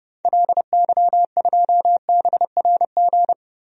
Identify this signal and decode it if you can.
Morse code, heard as a keyed tone, about 32 wpm, 700 Hz. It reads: LY2BRG
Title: DXCC-100 Call Signs - 6 Characters - Repeated 3x - 30wpm